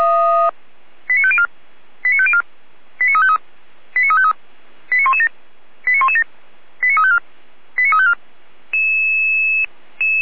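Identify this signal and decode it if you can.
Five-tone selective call (ZVEI), heard as quick runs of five tones, tones of about 70 ms. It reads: 89504 89504 89353 89353 89117 89117 89354 89354
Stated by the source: ZVEI five-tone selective call received via SDR.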